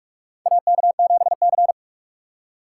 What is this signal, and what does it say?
Morse code, heard as a keyed tone, about 47 wpm, 690 Hz. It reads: AK7C
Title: USA Call Signs - 4 Characters - 8x Character Spacing - 45wpm